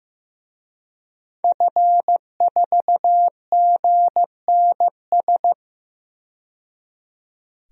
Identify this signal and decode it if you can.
Morse code, heard as a keyed tone, about 15 wpm, 700 Hz. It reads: F4GNS
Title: DXCC-100 Call Signs - 5 Characters - 15wpm